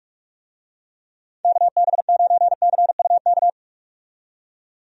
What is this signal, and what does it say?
Morse code, heard as a keyed tone, about 42 wpm, 700 Hz. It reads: KB9CUK
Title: USA Call Signs - 6 Characters - 45wpm